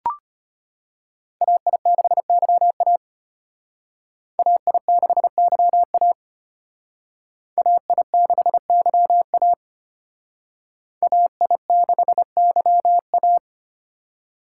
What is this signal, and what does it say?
Morse code, heard as a keyed tone, about 32 wpm, 700 Hz. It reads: AI6YA AI6YA AI6YA AI6YA